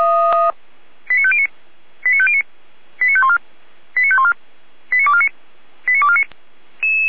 Five-tone selective call (ZVEI), heard as quick runs of five tones, tones of about 70 ms. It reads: 89509 89509 89514 89514 89249 89249
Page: feuerwehr-orlamuende